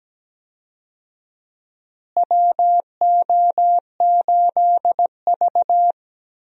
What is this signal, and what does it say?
Morse code, heard as a keyed tone, about 17 wpm, 700 Hz. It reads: WO8V
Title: USA Call Signs - 4 Characters - Repeated 2x - 17wpm